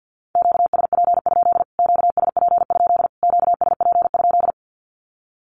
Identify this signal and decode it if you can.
Morse code, heard as a keyed tone, about 50 wpm, 700 Hz. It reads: QSP? QSP? QSP?
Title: Q-Codes Repeated 3x 50wpm